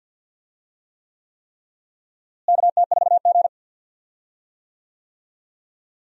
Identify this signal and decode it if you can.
Morse code, heard as a keyed tone, about 46 wpm, 700 Hz. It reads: KT4G
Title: USA Call Signs - 4 Characters - Repeated 3x - 50wpm